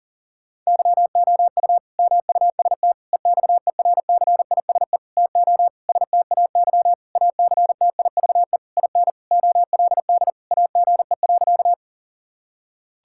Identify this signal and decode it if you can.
Morse code, heard as a keyed tone, about 40 wpm, 700 Hz. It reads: YOU MUST EXERCISE TO STAY ACTIVE IN OLD AGE.
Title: Sentences from Top 1000 Words - Review Part 6 - 40wpm